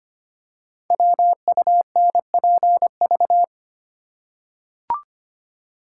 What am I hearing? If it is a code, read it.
Morse code, heard as a keyed tone, about 25 wpm, 700 Hz. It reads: WUNPV
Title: Code Groups - 5 Letters - 25wpm